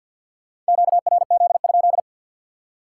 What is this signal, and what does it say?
Morse code, heard as a keyed tone, about 66 wpm, spaced as for 58 wpm, 700 Hz. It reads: QRZ?
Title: Speed-Racing Common QSO Phrases Encore - 50wpm